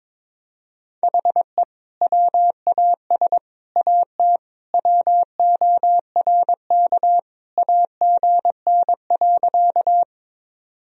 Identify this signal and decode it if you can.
Morse code, heard as a keyed tone, about 22 wpm, 700 Hz. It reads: HE WAS AT WORK AGN.
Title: Sentences from Top 200 Words and Abbreviations 22wpm